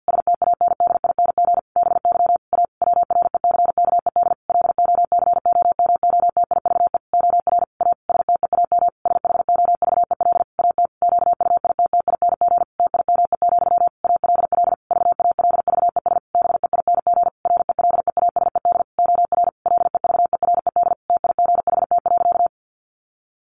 Morse code, heard as a keyed tone, about 48 wpm, 700 Hz. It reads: STANDING BY A WRECKED LOCOMOTIVE OR A STEAM SHOVEL AT QUITTING TIME, ALL VALVES BEING RELEASED OR LEVERED TIGHT.
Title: A Sound Of Thunder 50wpm